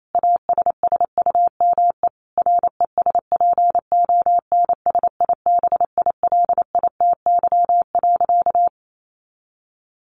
Morse code, heard as a keyed tone, about 28 wpm, 700 Hz. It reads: ASSUME RESPONSIBILITY.
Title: Sentences from Top 1500 Words 28wpm